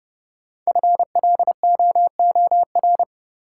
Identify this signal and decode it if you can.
Morse code, heard as a keyed tone, about 30 wpm, 700 Hz. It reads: FLOOR
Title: Sets of 3 Words - New Word Drill - 30wpm